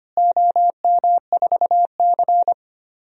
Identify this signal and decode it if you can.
Morse code, heard as a keyed tone, about 25 wpm, 700 Hz. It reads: OM4C
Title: DXCC-100 Call Signs - 4 Characters - Repeated 3x - 25wpm